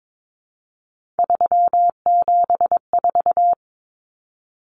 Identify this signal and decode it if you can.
Morse code, heard as a keyed tone, about 22 wpm, 700 Hz. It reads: 374
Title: Numbers - 3-Digit 22wpm